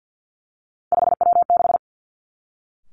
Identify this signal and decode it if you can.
Morse code, heard as a keyed tone, about 50 wpm, 700 Hz. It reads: 5W6
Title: Three Challenging Letter-Number Combinations 50wpm